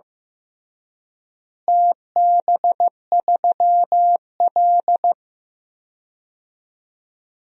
Morse code, heard as a keyed tone, about 15 wpm, 700 Hz. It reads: TB3L
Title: DXCC-100 Call Signs - 4 Characters - Repeated 3x - 15wpm